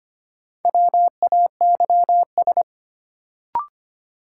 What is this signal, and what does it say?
Morse code, heard as a keyed tone, about 25 wpm, 700 Hz. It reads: WAYS
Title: Sets of 5 Words Encore - New Word Drill - 25wpm